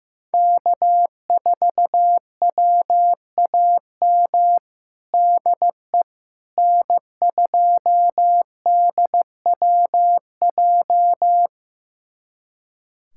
Morse code, heard as a keyed tone, about 15 wpm, 700 Hz. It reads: K4WAM DE N2DWJ
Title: USA Callsign DE Callsign - 5 Characters - 15wpm